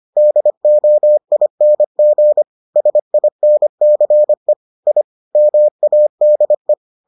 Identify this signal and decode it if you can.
Morse code, heard as a keyed tone, about 25 wpm, 600 Hz. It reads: DOING SINCE I MADE